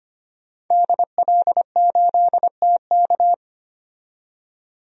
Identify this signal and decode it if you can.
Morse code, heard as a keyed tone, about 25 wpm, 700 Hz. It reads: DL8TK
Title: DXCC-100 Call Signs - 5 Characters - 25wpm